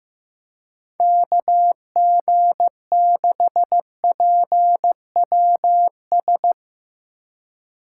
Morse code, heard as a keyed tone, about 15 wpm, 700 Hz. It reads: KG6PWS
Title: Speed Racing - USA 2x3 Call Signs 15wpm